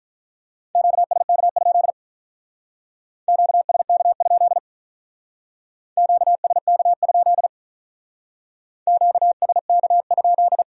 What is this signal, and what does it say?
Morse code, heard as a keyed tone, about 43 wpm, 700 Hz. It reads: QSK? QSK? QSK? QSK?
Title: Speed-Racing Q-Codes - 35wpm